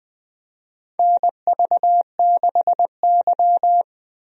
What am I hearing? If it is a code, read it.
Morse code, heard as a keyed tone, about 20 wpm, 700 Hz. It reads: NV6Y
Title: USA Call Signs - 4 Characters - Repeated 3x - 20wpm